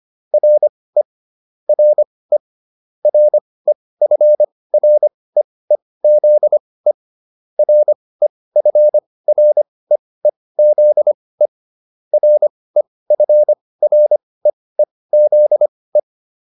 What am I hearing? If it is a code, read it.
Morse code, heard as a keyed tone, about 25 wpm, spaced as for 18 wpm, 600 Hz. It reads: RE RE REFREEZE REFREEZE REFREEZE